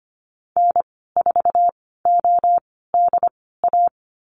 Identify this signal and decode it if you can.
Morse code, heard as a keyed tone, about 25 wpm, spaced as for 16 wpm, 700 Hz. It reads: N4ODA
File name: Lesson 15g - Number 5 - Call Signs 25wpm